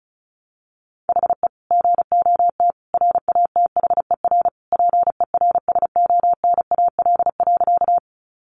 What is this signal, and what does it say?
Morse code, heard as a keyed tone, about 34 wpm, 700 Hz. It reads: HE GOT RATHER PERSONAL.
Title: Sentences from Top 700 Words 35wpm